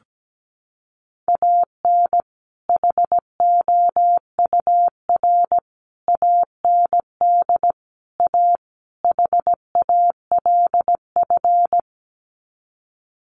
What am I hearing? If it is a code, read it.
Morse code, heard as a keyed tone, about 17 wpm, 700 Hz. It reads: AN HOUR AND A HALF
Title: Sets of 5 Words 17wpm